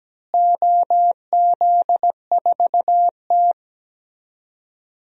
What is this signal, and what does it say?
Morse code, heard as a keyed tone, about 17 wpm, 700 Hz. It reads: OZ4T
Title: DXCC-100 Call Signs - 4 Characters - Repeated 2x - 17wpm